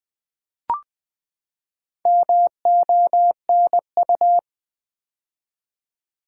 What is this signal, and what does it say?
Morse code, heard as a keyed tone, about 20 wpm, 700 Hz. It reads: MONU
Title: Build A Word - 4 Letters At A Time - 8-Letter Words - 20wpm